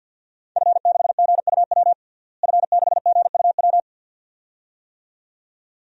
Morse code, heard as a keyed tone, about 52 wpm, 700 Hz. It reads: F6GUW F6GUW
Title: DXCC-100 Call Signs - 5 Characters - Repeated 2x - 50wpm